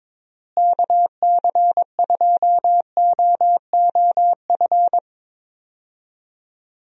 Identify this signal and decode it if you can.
Morse code, heard as a keyed tone, about 22 wpm, 700 Hz. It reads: KC2OOF
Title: USA Call Signs - 6 Characters - Repeated 3x - 22wpm